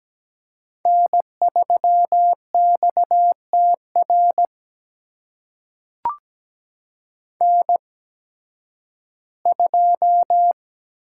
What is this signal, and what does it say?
Morse code, heard as a keyed tone, about 17 wpm, spaced as for 11 wpm, 700 Hz. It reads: N3XTR N2
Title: USA Call Signs - 5 Characters - 8x Character Spacing - 17wpm